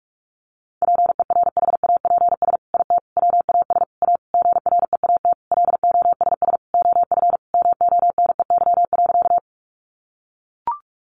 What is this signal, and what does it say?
Morse code, heard as a keyed tone, about 44 wpm, 700 Hz. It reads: PERHAPS IT WAS A GREAT LOSS OF MONEY.
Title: Sentences from Top 1400 Words 45wpm